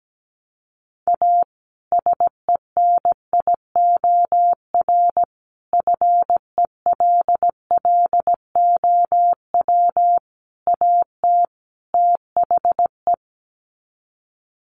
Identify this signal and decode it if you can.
Morse code, heard as a keyed tone, about 17 wpm, 700 Hz. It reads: A SENIOR FELLOW AT THE
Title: Sets of 5 Words Encore 17wpm